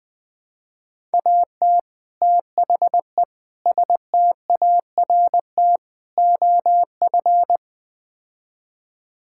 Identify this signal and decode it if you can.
Morse code, heard as a keyed tone, about 20 wpm, 700 Hz. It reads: AT THE START OF
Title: Build A Phrase - 5-Word Phrases - 20wpm